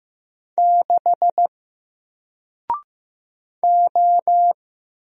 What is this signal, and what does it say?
Morse code, heard as a keyed tone, about 15 wpm, 700 Hz. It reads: 6 O
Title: Single Letter-Number Alt 15wpm